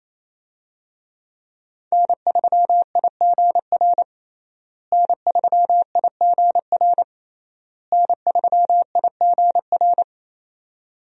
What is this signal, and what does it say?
Morse code, heard as a keyed tone, about 28 wpm, 700 Hz. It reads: N3IGR N3IGR N3IGR
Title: USA Call Signs - 5 Characters - Repeated 3x - 28wpm